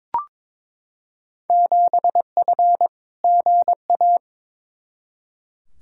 Morse code, heard as a keyed tone, about 22 wpm, 700 Hz. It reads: 7F GA
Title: Field Day Exchanges Encore - 22wpm